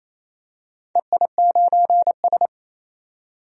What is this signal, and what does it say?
Morse code, heard as a keyed tone, about 28 wpm, 700 Hz. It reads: EI9S
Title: DXCC-100 Call Signs - 4 Characters - 28wpm